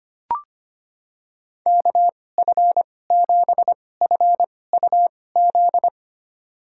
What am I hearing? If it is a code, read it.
Morse code, heard as a keyed tone, about 25 wpm, spaced as for 18 wpm, 700 Hz. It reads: KF7FUZ
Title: USA Call Signs - 6 Characters - 2x Character Spacing - 25wpm